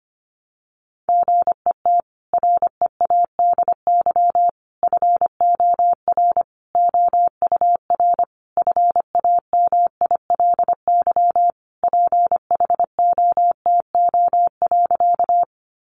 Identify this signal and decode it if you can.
Morse code, heard as a keyed tone, about 25 wpm, 700 Hz. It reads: GET READY FOR OUR FAMILY PHOTO.
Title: Sentences from Top 1200 Words 25wpm